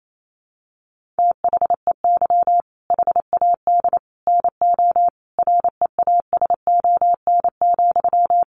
Morse code, heard as a keyed tone, about 28 wpm, 700 Hz. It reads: THEY HAD NO REASON,
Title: Sentences from Top 1500 Words 28wpm